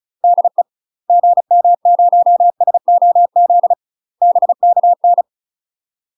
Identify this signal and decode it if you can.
Morse code, heard as a keyed tone, about 36 wpm, 700 Hz. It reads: DE GM0SOZ BKN
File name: UK MASTER QSOs TRAINING / 35WPM / UK MASTER QSOs 0033